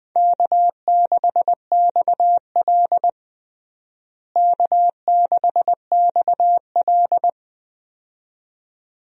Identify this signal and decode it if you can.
Morse code, heard as a keyed tone, about 20 wpm, 700 Hz. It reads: K6XL K6XL
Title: USA Call Signs - 4 Characters - Repeated 2x - 20wpm